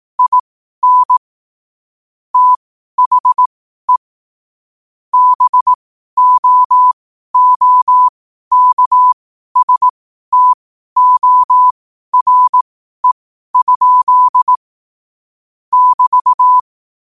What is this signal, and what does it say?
Morse code, heard as a keyed tone, about 18 wpm, spaced as for 12 wpm, 1000 Hz. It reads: IN THE BOOKSTORE? <BT>
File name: qotdfiles / 12wpm